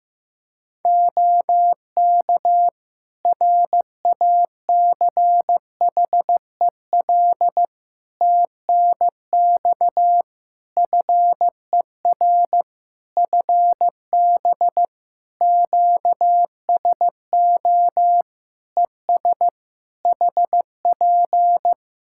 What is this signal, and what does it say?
Morse code, heard as a keyed tone, about 15 wpm, 700 Hz. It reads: OK RACHEL TNX FER FB QSO ES HP
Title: K1USN SST QSO Practice 15wpm